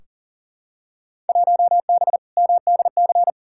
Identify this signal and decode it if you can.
Morse code, heard as a keyed tone, about 39 wpm, 700 Hz. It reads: 1B MDC
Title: Field Day Exchanges - 40wpm